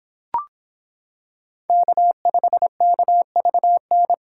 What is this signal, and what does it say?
Morse code, heard as a keyed tone, about 26 wpm, 700 Hz. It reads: K5KVN